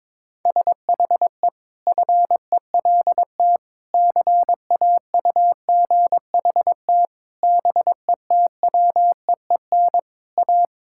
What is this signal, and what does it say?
Morse code, heard as a keyed tone, about 22 wpm, 700 Hz. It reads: SHE FELT CAUGHT BETWEEN A
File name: practice / 22wpm / Sentences with 8-Word Phrases - 22wpm